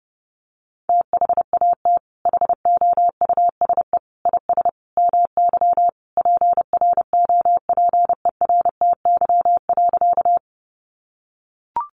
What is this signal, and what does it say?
Morse code, heard as a keyed tone, about 30 wpm, 700 Hz. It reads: THAT HOUSE IS MY PROPERTY.
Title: Sentences from Top 1000 Words 30wpm